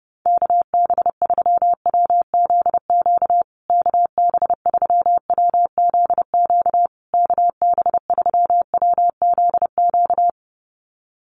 Morse code, heard as a keyed tone, about 30 wpm, 700 Hz. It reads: KB3WZQ KB3WZQ KB3WZQ
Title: USA Call Signs Repeated 3x 30wpm